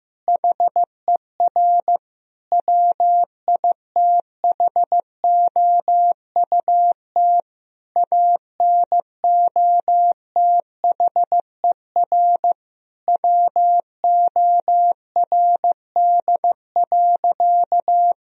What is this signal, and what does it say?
Morse code, heard as a keyed tone, about 16 wpm, 700 Hz. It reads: HER WITHOUT ANOTHER WORD.
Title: Sentences from Top 1000 Words - Review Part 3 - 15wpm